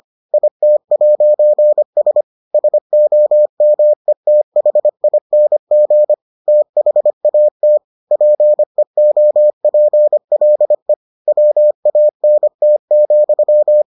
Morse code, heard as a keyed tone, about 25 wpm, 600 Hz. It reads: IT'S SOMETHING THAT PEOPLE WANT,